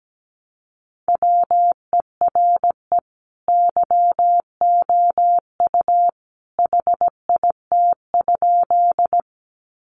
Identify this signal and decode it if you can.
Morse code, heard as a keyed tone, about 17 wpm, 700 Hz. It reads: WERE YOU HIT?
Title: Sentences from Top 1200 Words 17wpm